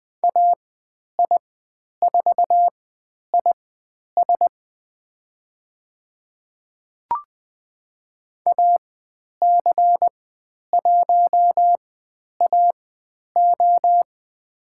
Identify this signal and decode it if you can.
Morse code, heard as a keyed tone, about 20 wpm, spaced as for 10 wpm, 700 Hz. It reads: AI4IS AC1AO
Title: 109 Lesson 18g - Letter m - Call Signs 20wpm